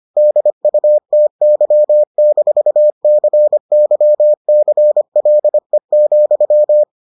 Morse code, heard as a keyed tone, about 25 wpm, 600 Hz. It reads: DUTY-CYCLE,